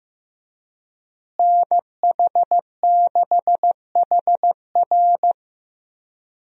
Morse code, heard as a keyed tone, about 15 wpm, 700 Hz. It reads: NH6HR
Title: USA Call Signs - 5 Characters - Repeated 3x - 15wpm